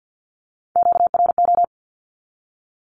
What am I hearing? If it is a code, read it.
Morse code, heard as a keyed tone, about 50 wpm, 700 Hz. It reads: QRO
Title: Q-Codes with Spoken Meaning - 50wpm